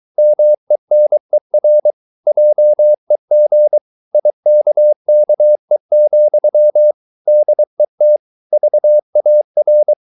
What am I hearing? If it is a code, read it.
Morse code, heard as a keyed tone, about 23 wpm, 600 Hz. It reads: MENER JEG IKKE, DET VAR
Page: Ferie 23 wpm